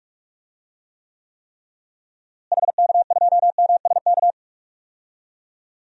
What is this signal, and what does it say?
Morse code, heard as a keyed tone, about 47 wpm, 700 Hz. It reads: HK2MSK
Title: DXCC-100 Call Signs - 6 Characters - Repeated 3x - 45wpm